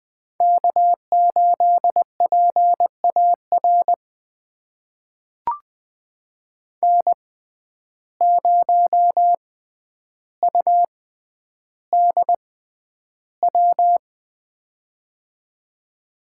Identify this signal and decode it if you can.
Morse code, heard as a keyed tone, about 20 wpm, spaced as for 10 wpm, 700 Hz. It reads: K8PAR N0UDW
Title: USA Call Signs - 5 Characters - 6x Character Spacing - 20wpm